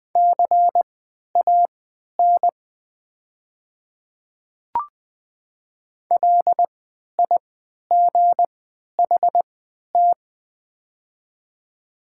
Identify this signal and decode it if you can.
Morse code, heard as a keyed tone, about 20 wpm, spaced as for 11 wpm, 700 Hz. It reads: CAN LIGHT
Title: 3x Character Spacing - Top 200 Words - 20wpm